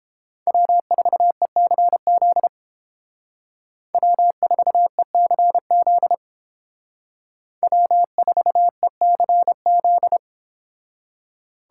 Morse code, heard as a keyed tone, about 31 wpm, 700 Hz. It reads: W4ECZ W4ECZ W4ECZ